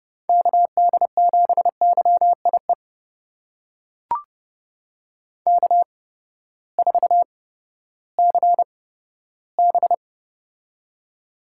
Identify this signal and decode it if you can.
Morse code, heard as a keyed tone, about 32 wpm, spaced as for 15 wpm, 700 Hz. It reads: KD7YIE K4CB
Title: USA Call Signs - 8x Character Spacing - 30wpm